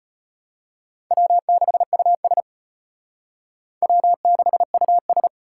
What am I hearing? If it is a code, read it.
Morse code, heard as a keyed tone, about 32 wpm, 700 Hz. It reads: W6US W6US